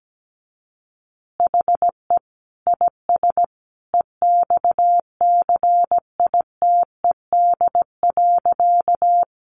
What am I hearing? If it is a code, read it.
Morse code, heard as a keyed tone, about 17 wpm, 700 Hz. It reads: HE IS EXCITED.
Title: Additional Easy Phrases 17wpm